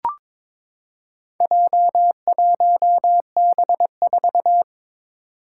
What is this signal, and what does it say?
Morse code, heard as a keyed tone, about 22 wpm, 700 Hz. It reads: J1B4
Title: Four Challenging Letter-Number Combinations 22wpm